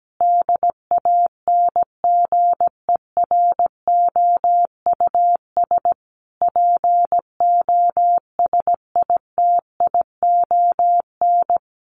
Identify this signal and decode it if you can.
Morse code, heard as a keyed tone, about 17 wpm, 700 Hz. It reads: DANGEROUS POSITION
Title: Sentences from Top 1400 Words 17wpm